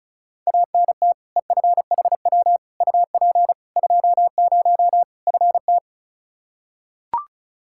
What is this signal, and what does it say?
Morse code, heard as a keyed tone, about 36 wpm, 700 Hz. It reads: ANT EFHW UP 20 FT
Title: Practice QSO - Exchange 2 - Info - 35wpm